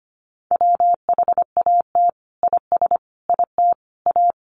Morse code, heard as a keyed tone, about 25 wpm, 700 Hz. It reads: WHAT IS IT A
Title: Sentences from Top 1300 Words 25wpm